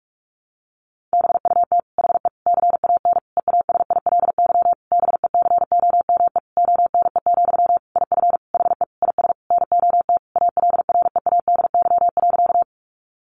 Morse code, heard as a keyed tone, about 44 wpm, 700 Hz. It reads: BUT HE CAN EASILY BECOME ONE, IF HE IS NOT ALREADY.
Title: Sentences from Top 900 Words 45wpm